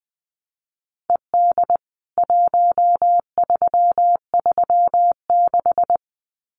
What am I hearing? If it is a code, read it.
Morse code, heard as a keyed tone, about 19 wpm, 700 Hz. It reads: ED 1336
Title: CWT Exchange 20wpm